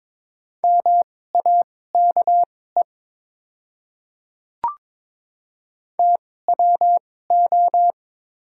Morse code, heard as a keyed tone, about 22 wpm, spaced as for 16 wpm, 700 Hz. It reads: MAKE TWO
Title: 2x Character Spacing - Top 200 Words - 22wpm